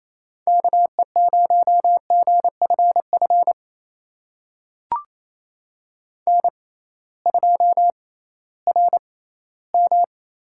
Morse code, heard as a keyed tone, about 28 wpm, spaced as for 16 wpm, 700 Hz. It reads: KE0GFF N2RM
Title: USA Call Signs - 6x Character Spacing - 28wpm